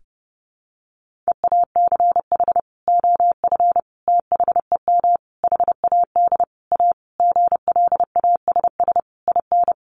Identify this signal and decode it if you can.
Morse code, heard as a keyed tone, about 30 wpm, 700 Hz. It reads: EACH OF THEM HAD A GLASS IN
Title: Sentences from Top 1300 Words 30wpm